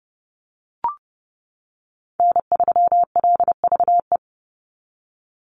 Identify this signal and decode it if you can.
Morse code, heard as a keyed tone, about 30 wpm, 700 Hz. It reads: N3LVE
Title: USA Call Signs 30wpm